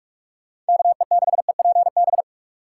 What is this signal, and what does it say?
Morse code, heard as a keyed tone, about 58 wpm, spaced as for 51 wpm, 700 Hz. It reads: KE6EPB
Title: Speed Racing - USA 2x3 Call Signs 45wpm